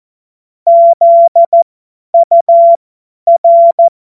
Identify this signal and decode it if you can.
Morse code, heard as a keyed tone, about 14 wpm, spaced as for 10 wpm, 680 Hz. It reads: ZUR
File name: Morsekurs / Witze / Witz 02. November 2021, 50 F70